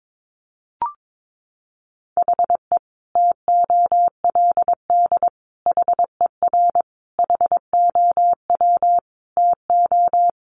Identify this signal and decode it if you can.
Morse code, heard as a keyed tone, about 22 wpm, 700 Hz. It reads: HE TOLD HER HOW TO
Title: Sentences from Top 1500 Words 22wpm